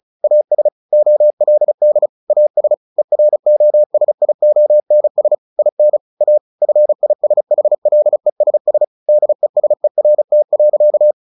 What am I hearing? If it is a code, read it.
Morse code, heard as a keyed tone, about 34 wpm, 600 Hz. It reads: AS OLD AS EROSIONS IN A FISHLESS DESERT.
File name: the old man and the sea / the old man and the sea 1 35wpm